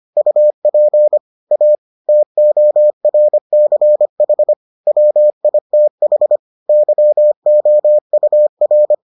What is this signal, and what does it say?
Morse code, heard as a keyed tone, about 25 wpm, 600 Hz. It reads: UP A TORCH WITH YOUR